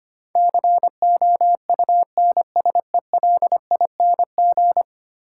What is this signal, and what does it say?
Morse code, heard as a keyed tone, about 25 wpm, 700 Hz. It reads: COUNSELING
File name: 10 Letter Words - First 3 Letters Given - 25wpm